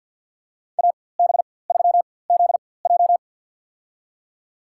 Morse code, heard as a keyed tone, about 51 wpm, spaced as for 23 wpm, 700 Hz. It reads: AB3ZJ
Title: USA Call Signs - 5 Characters - 4x Character Spacing - 50wpm